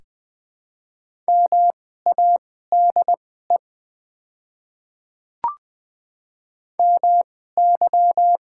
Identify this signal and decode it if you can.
Morse code, heard as a keyed tone, about 20 wpm, spaced as for 15 wpm, 690 Hz. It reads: MADE MY
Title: 2x Character Spacing - Top 200 Words - 20wpm